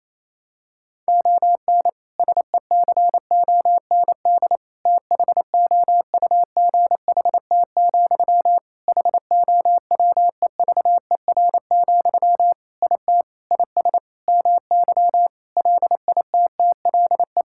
Morse code, heard as a keyed tone, about 28 wpm, 700 Hz. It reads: ON SECOND THOUGHT, HOWEVER, IT IS MY LITTLE
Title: Sentences from Top 1600 Words 28wpm